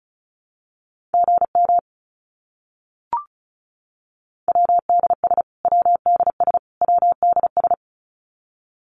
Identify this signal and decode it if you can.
Morse code, heard as a keyed tone, about 34 wpm, 700 Hz. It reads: GM WDS WDS WDS
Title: Abbreviations Repeated 3x - 35wpm